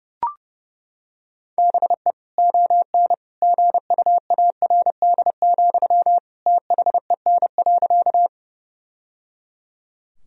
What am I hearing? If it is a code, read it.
Morse code, heard as a keyed tone, about 30 wpm, 700 Hz. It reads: BE ON GUARD, THEN.
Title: Sentences from Top 1000 Words - Review Part 4 - 30wpm